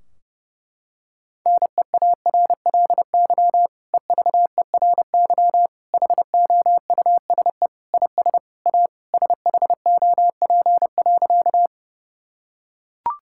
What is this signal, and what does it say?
Morse code, heard as a keyed tone, about 30 wpm, 700 Hz. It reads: NEARLY EVERY HOUSE IS A SHOP.
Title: Sentences from Top 1600 Words 30wpm